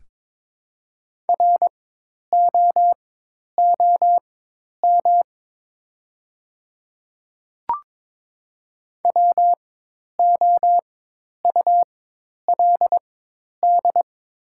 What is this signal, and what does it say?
Morse code, heard as a keyed tone, about 22 wpm, spaced as for 10 wpm, 700 Hz. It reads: ROOM WOULD
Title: 4x Character Spacing - Top 300 Words - 22wpm